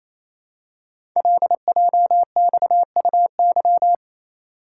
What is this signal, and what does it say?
Morse code, heard as a keyed tone, about 29 wpm, 700 Hz. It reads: LJXUY
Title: Code Groups - 5 Letter:Numbers - 28wpm